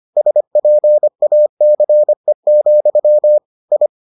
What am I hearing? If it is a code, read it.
Morse code, heard as a keyed tone, about 25 wpm, 600 Hz. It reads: SPACE, I